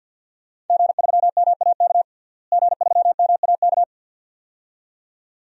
Morse code, heard as a keyed tone, about 51 wpm, 700 Hz. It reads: G3MAK G3MAK
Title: DXCC-100 Call Signs - 5 Characters - Repeated 2x - 50wpm